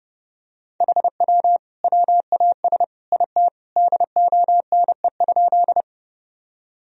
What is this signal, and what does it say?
Morse code, heard as a keyed tone, about 32 wpm, 700 Hz. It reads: HW WAS IT DONE?
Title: Sentences from Top 500 Words and Abbreviations 30wpm